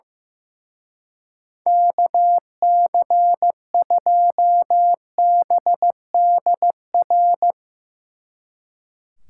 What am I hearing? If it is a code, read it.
Morse code, heard as a keyed tone, about 16 wpm, 700 Hz. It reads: KC2BDR
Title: Speed Racing - USA 2x3 Call Signs 15wpm